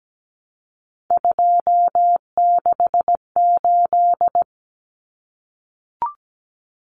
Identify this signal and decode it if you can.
Morse code, heard as a keyed tone, about 17 wpm, 700 Hz. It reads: 268
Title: Common Cut Numbers - 3-Digit 17wpm